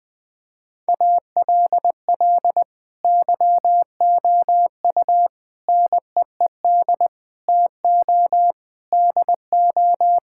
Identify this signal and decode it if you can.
Morse code, heard as a keyed tone, about 21 wpm, 700 Hz. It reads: ALL YOU NEED TO DO
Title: Sentences with 6-Word Phrases - 20wpm